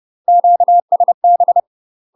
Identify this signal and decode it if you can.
Morse code, heard as a keyed tone, about 30 wpm, 700 Hz. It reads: QSB
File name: UK MASTER QSOs TRAINING / 30WPM / UK MASTER QSOs 0000